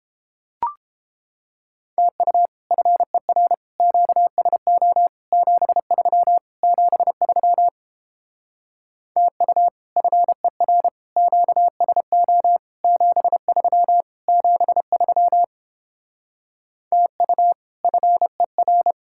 Morse code, heard as a keyed tone, about 29 wpm, 700 Hz. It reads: TU FER QSO 73 73 TU FER QSO 73 73 TU FER